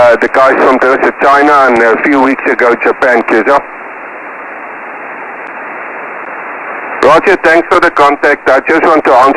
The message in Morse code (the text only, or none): none